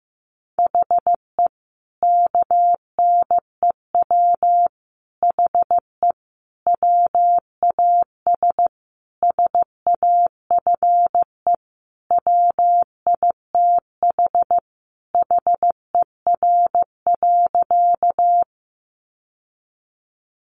Morse code, heard as a keyed tone, about 15 wpm, 700 Hz. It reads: HE KNEW HE WAS SAFE WITH HER.
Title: Sentences from Top 900 Words 15wpm